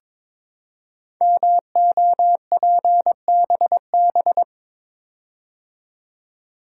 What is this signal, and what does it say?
Morse code, heard as a keyed tone, about 22 wpm, 700 Hz. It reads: MOPBB
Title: Code Groups - 5 Letters - 22wpm